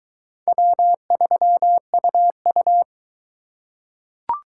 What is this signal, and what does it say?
Morse code, heard as a keyed tone, about 21 wpm, 700 Hz. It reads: W3UU